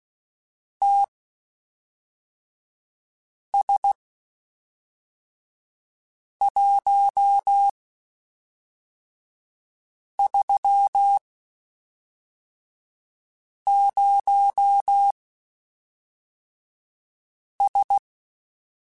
Morse code, heard as a keyed tone, about 16 wpm, spaced as for 3 wpm, 800 Hz. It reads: TS130S
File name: (spaced) / QSOs Spaced-mp3s / WD0BGP 1 @5 sp